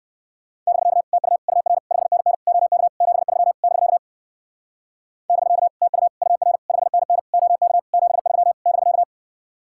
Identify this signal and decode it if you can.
Morse code, heard as a keyed tone, about 59 wpm, 700 Hz. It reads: <BK> TU UR 5NN OK 73 <BK> <BK> TU UR 5NN OK 73 <BK>